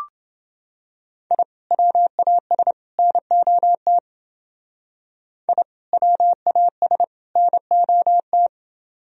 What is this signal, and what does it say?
Morse code, heard as a keyed tone, about 29 wpm, 700 Hz. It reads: I WAS NOT I WAS NOT